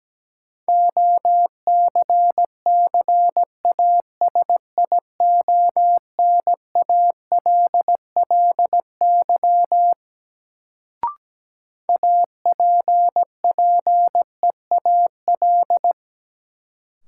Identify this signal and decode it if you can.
Morse code, heard as a keyed tone, about 17 wpm, 700 Hz. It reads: OCCASIONALLY APPEAL
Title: Sentences from Top 1300 Words - New Word Drill - 17wpm